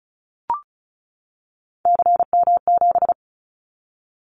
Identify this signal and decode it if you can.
Morse code, heard as a keyed tone, about 34 wpm, 700 Hz. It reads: CM7
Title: Callsign Prefixes for Top 100 DXCC - 35wpm